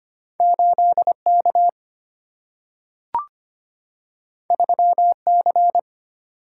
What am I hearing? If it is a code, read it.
Morse code, heard as a keyed tone, about 25 wpm, 690 Hz. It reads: 8K 3C
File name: Two Letter-Number Combinations - 25wpm